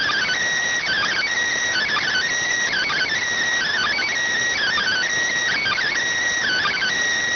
Emergency Alert System (SAME) tones (header bursts), heard as harsh buzzing bursts. It is heard from 1.2 to 6.9 s.